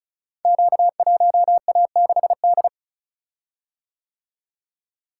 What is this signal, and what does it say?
Morse code, heard as a keyed tone, about 36 wpm, 700 Hz. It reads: Q1A6D
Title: Code Groups - 5 Letter:Numbers - 35wpm